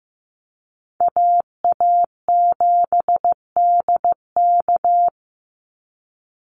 Morse code, heard as a keyed tone, about 15 wpm, 700 Hz. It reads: AA7DK
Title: NA SOTA Chaser Callsigns 15wpm